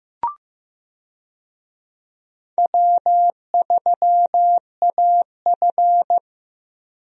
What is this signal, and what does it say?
Morse code, heard as a keyed tone, about 15 wpm, 700 Hz. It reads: W3AF
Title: USA Call Signs - 4 Characters - Repeated 2x - 15wpm